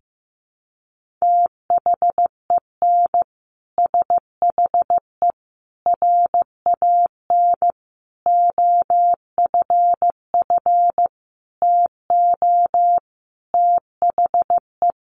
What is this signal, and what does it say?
Morse code, heard as a keyed tone, about 15 wpm, 700 Hz. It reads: THEN SHE RAN OFF TO THE
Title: Sentences from Top 1400 Words 15wpm